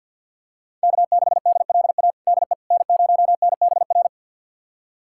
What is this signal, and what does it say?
Morse code, heard as a keyed tone, about 50 wpm, 690 Hz. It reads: K6DLA DE N0NBR
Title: USA Callsign DE Callsign - 5 Characters - 50wpm